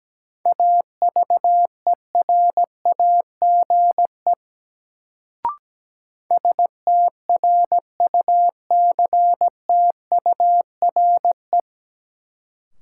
Morse code, heard as a keyed tone, about 17 wpm, 700 Hz. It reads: AVERAGE STRUCTURE
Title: Sentences from Top 1000 Words - Review Part 6 - New Word Drill - 17wpm